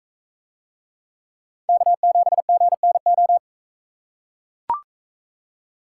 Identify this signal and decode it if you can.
Morse code, heard as a keyed tone, about 40 wpm, 700 Hz. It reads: K7GNO